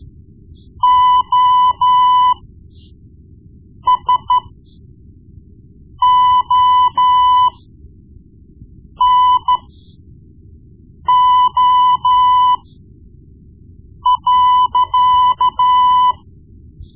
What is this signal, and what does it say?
Morse code, heard as a keyed tone, about 13 wpm, spaced as for 5 wpm, 1000 Hz. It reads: OSONO.
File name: Morse Lessons / Morse Lessons / Tape-1B